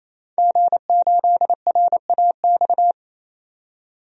Morse code, heard as a keyed tone, about 28 wpm, 700 Hz. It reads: G8RAX
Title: DXCC-100 Call Signs - 5 Characters - 28wpm